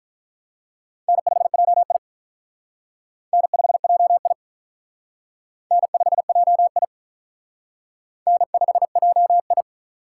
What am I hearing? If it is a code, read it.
Morse code, heard as a keyed tone, about 44 wpm, 700 Hz. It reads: N5JI N5JI N5JI N5JI